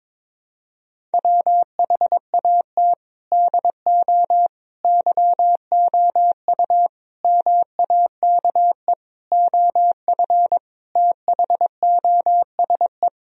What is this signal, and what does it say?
Morse code, heard as a keyed tone, about 22 wpm, 700 Hz. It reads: WHAT DO YOU MAKE OF THOSE
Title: Sentences from Top 1000 Words - Review Part 1 - 22wpm